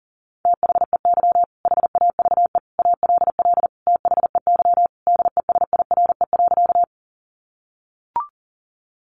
Morse code, heard as a keyed tone, about 40 wpm, 700 Hz. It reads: THEY HAVE ALL THEY DESIRE.
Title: Sentences from Top 1100 Words 40wpm